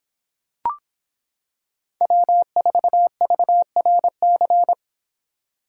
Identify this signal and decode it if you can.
Morse code, heard as a keyed tone, about 26 wpm, 700 Hz. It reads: W4VRC